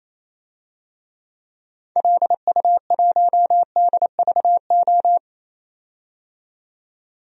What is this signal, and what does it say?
Morse code, heard as a keyed tone, about 28 wpm, 700 Hz. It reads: LU1DVO
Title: DXCC-100 Call Signs - 6 Characters - 28wpm